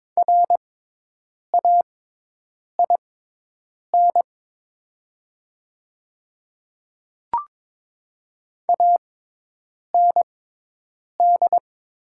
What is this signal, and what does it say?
Morse code, heard as a keyed tone, about 21 wpm, spaced as for 8 wpm, 700 Hz. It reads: RAIN AND
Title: 6x Character Spacing - Top 300 Words - 22wpm